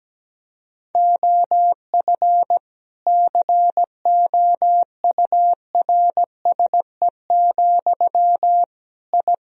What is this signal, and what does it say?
Morse code, heard as a keyed tone, about 17 wpm, 700 Hz. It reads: OF COURSE, I
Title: Sentences from Top 1000 Words - Review Part 4 - 17wpm